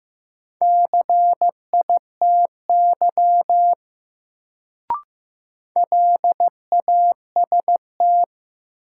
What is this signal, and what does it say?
Morse code, heard as a keyed tone, about 16 wpm, 700 Hz. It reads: CITY LAST
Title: Sentences from Top 300 Words - New Word Drill - 15wpm